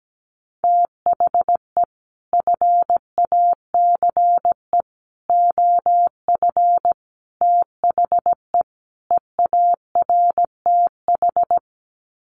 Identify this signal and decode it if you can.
Morse code, heard as a keyed tone, about 17 wpm, 700 Hz. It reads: THE FACE OF THE EARTH
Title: Sets of 5 Words Encore 17wpm